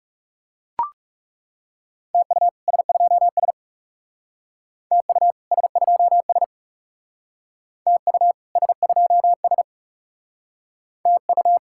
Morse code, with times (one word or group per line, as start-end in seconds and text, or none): 2.1-2.5 TU
2.7-3.5 S2S
4.9-5.3 TU
5.5-6.4 S2S
7.9-8.3 TU
8.6-9.6 S2S
11.1-11.6 TU